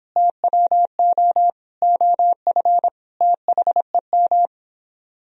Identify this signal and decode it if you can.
Morse code, heard as a keyed tone, about 26 wpm, 700 Hz. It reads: TWO OF THEM